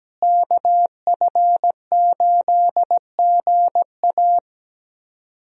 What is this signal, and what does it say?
Morse code, heard as a keyed tone, about 17 wpm, 700 Hz. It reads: KF8GA
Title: USA Call Signs - 5 Characters - Repeated 2x - 17wpm